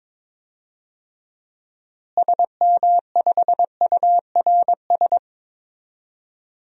Morse code, heard as a keyed tone, about 22 wpm, 700 Hz. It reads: SM5URS
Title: DXCC-100 Call Signs - 6 Characters - Repeated 3x - 22wpm